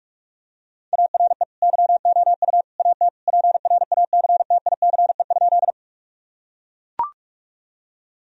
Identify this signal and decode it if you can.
Morse code, heard as a keyed tone, about 41 wpm, 700 Hz. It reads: ARE YOU AT PRACTICE?
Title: Sentences from Top 1000 Words - Review Part 4 - 45wpm